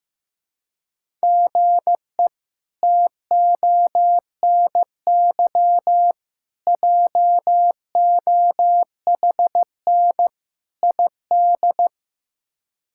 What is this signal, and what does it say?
Morse code, heard as a keyed tone, about 16 wpm, 700 Hz. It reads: GE TONY JOHN ID
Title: K1USN SST QSO Practice - 15wpm